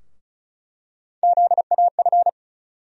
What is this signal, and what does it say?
Morse code, heard as a keyed tone, about 36 wpm, 700 Hz. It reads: ZAF
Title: USA Call Sign Postfixes - 3 Letters 35wpm